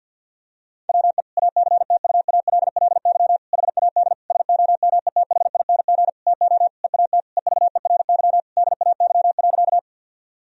Morse code, heard as a keyed tone, about 52 wpm, 700 Hz. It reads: WE ACTUALLY HAD SOMETHING TO EAT EVERY DAY.
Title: Sentences from Top 1000 Words - Review Part 4 - 50wpm